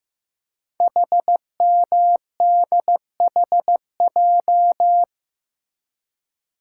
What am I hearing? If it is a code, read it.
Morse code, heard as a keyed tone, about 15 wpm, 700 Hz. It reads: HMDHJ
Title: Code Groups - 5 Letters - 15wpm